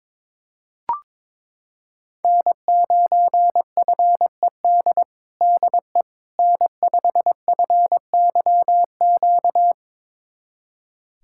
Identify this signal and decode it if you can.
Morse code, heard as a keyed tone, about 22 wpm, 700 Hz. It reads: N9FED DE N5FYQ
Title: USA Callsign DE Callsign - 5 Characters - 22wpm